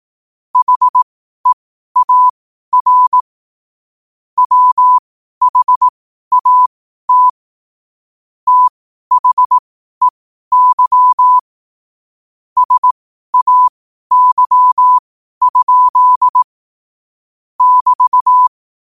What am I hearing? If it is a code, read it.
Morse code, heard as a keyed tone, about 18 wpm, spaced as for 12 wpm, 1000 Hz. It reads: HEAR WHAT THEY SAY? <BT>
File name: qotdfiles / 12wpm